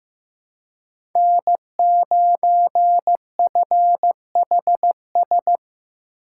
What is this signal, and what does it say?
Morse code, heard as a keyed tone, about 15 wpm, 700 Hz. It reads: N9FHS
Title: USA Call Signs - 5 Characters - 4x Character Spacing - 15wpm